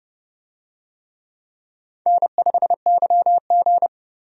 Morse code, heard as a keyed tone, about 32 wpm, 700 Hz. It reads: N5YG
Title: USA Call Signs - 4 Characters - 30wpm